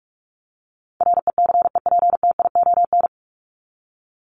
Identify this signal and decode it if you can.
Morse code, heard as a keyed tone, about 44 wpm, 700 Hz. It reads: RECEPTION
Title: Top 1751-2000 Words 45wpm